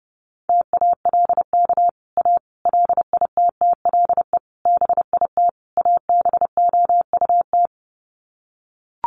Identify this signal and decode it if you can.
Morse code, heard as a keyed tone, about 30 wpm, 700 Hz. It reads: TALK A LITTLE BIT ABOUT
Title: Sets of 5 Words Encore 30wpm